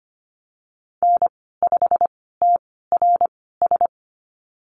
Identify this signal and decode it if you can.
Morse code, heard as a keyed tone, about 25 wpm, spaced as for 16 wpm, 700 Hz. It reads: N5TRS
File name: Lesson 15g - Number 5 - Call Signs 25wpm